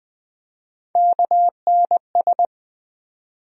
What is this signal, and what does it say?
Morse code, heard as a keyed tone, about 20 wpm, 700 Hz. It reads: KNS
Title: USA Call Sign Postfixes - 3 Letters 20wpm